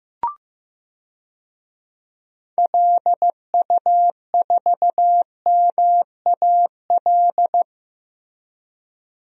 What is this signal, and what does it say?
Morse code, heard as a keyed tone, about 15 wpm, 700 Hz. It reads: LU4MAL
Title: DXCC-100 Call Signs - 6 Characters - Repeated 3x - 15wpm